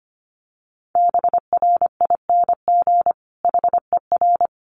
Morse code, heard as a keyed tone, about 25 wpm, 700 Hz. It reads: BRING HER
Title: Sentences from Top 1000 Words 25wpm